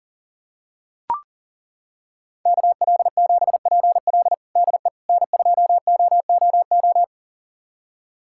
Morse code, heard as a keyed tone, about 37 wpm, 700 Hz. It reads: KL7PL DE N2OOO
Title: USA Callsign DE Callsign - 5 Characters - 40wpm